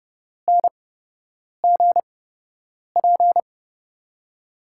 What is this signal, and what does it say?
Morse code, heard as a keyed tone, about 32 wpm, spaced as for 8 wpm, 700 Hz. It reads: NGP
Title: USA Call Signs - 8x Character Spacing - 30wpm